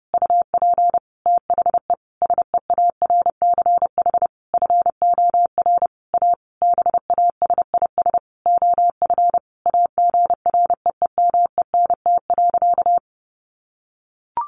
Morse code, heard as a keyed tone, about 30 wpm, 700 Hz. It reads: UP THE SEARCH FOR A BASIS OF AGREEMENT.
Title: Sentences from Top 1500 Words 30wpm